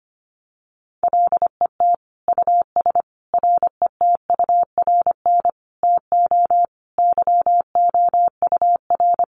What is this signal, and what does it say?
Morse code, heard as a keyed tone, about 25 wpm, 700 Hz. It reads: LET US RETURN TO YOUR
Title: Sentences from Top 800 Words 25wpm